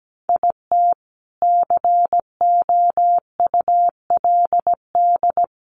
Morse code, heard as a keyed tone, about 17 wpm, 700 Hz. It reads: IT COULD
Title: Sentences from Top 1200 Words 17wpm